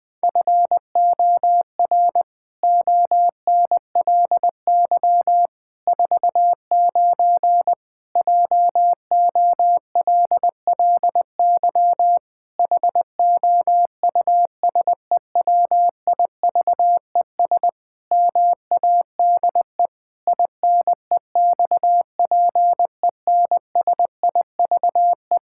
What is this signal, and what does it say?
Morse code, heard as a keyed tone, about 20 wpm, 700 Hz. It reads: FOR ONLY 49 JOLLY HOUSEWIVES MADE INEXPENSIVE
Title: Pangram Challenge v2 - 20wpm